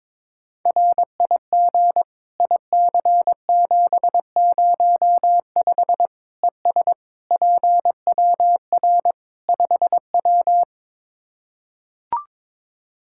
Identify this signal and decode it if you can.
Morse code, heard as a keyed tone, about 22 wpm, 700 Hz. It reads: RIG IC705 ES PWR 5W
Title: Practice QSO - Exchange 2 - Info - 22wpm